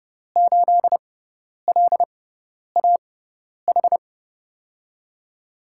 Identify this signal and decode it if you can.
Morse code, heard as a keyed tone, about 30 wpm, spaced as for 10 wpm, 700 Hz. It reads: 8LAH
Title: USA Call Signs - 6x Character Spacing - 30wpm